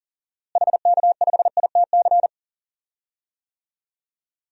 Morse code, heard as a keyed tone, about 40 wpm, 700 Hz. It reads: HK5ITC
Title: DXCC-100 Call Signs - 6 Characters - Repeated 2x - 40wpm